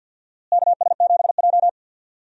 Morse code, heard as a keyed tone, about 50 wpm, 700 Hz. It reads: KS7J
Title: USA Call Signs - 4 Characters - Repeated 2x - 50wpm